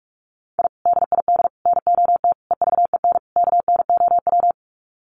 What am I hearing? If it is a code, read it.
Morse code, heard as a keyed tone, about 44 wpm, 700 Hz. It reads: I DID NOT EVEN KNOW
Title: Sets of 5 Words Encore 45wpm